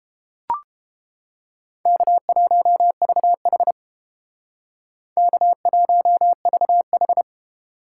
Morse code, heard as a keyed tone, about 31 wpm, 700 Hz. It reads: K1VH K1VH